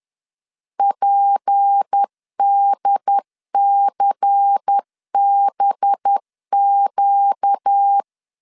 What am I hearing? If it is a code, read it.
Morse code, heard as a keyed tone, about 10 wpm, 800 Hz. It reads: PDCBQ
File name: CW / DET / 1 6 1500C S B 10wpm 4137